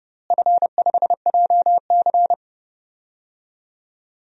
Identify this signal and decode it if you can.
Morse code, heard as a keyed tone, about 30 wpm, 700 Hz. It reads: F5JC
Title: DXCC-100 Call Signs - 4 Characters - Repeated 3x - 30wpm